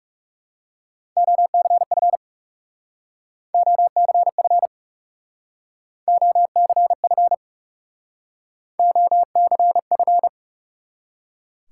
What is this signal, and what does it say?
Morse code, heard as a keyed tone, about 36 wpm, 700 Hz. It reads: OCF OCF OCF OCF
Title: Speed-Racing Top 180 QSO Elements 30wpm